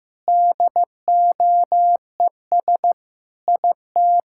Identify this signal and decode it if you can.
Morse code, heard as a keyed tone, about 16 wpm, 700 Hz. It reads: DOES IT
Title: Sentences from Top 1000 Words - Review Part 5 - 15wpm